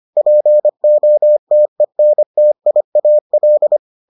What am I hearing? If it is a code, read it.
Morse code, heard as a keyed tone, about 25 wpm, 600 Hz. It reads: POTENTIAL